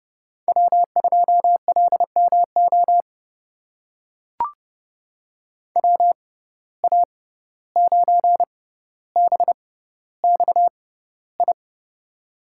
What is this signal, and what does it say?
Morse code, heard as a keyed tone, about 31 wpm, spaced as for 15 wpm, 700 Hz. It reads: W2LMO WA9BXI
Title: USA Call Signs - 6x Character Spacing - 30wpm